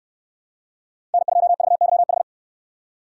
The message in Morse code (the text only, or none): N2VYH